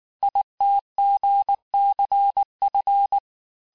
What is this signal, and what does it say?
Morse code, heard as a keyed tone, about 19 wpm, 800 Hz. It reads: ITGCF
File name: cours-du-lundi files / cours-1589-texte-chiffre-vitesse-18mpm